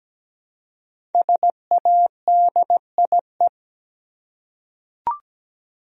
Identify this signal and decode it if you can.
Morse code, heard as a keyed tone, about 17 wpm, 700 Hz. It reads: SADIE
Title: USA Common Names - 17wpm